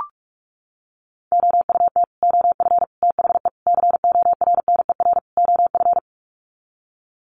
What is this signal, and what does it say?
Morse code, heard as a keyed tone, about 44 wpm, 700 Hz. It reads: OUT OF THE CORNER OF
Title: Sets of 5 Words 45wpm